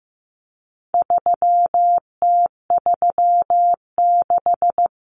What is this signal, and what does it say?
Morse code, heard as a keyed tone, about 15 wpm, 700 Hz. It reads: 3T36
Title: Common Cut Numbers - 4-Digit 15wpm